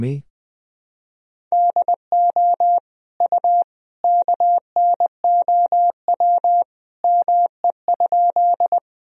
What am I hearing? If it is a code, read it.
Morse code, heard as a keyed tone, about 20 wpm, 700 Hz. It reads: DO U KNOW ME?
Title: Sentences from Top 200 Words and Abbreviations 20wpm